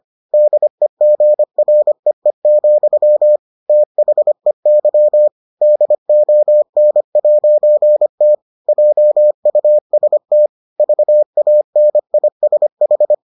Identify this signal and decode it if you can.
Morse code, heard as a keyed tone, about 25 wpm, 600 Hz. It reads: DEGREE, THEY DON'T JUST VANISH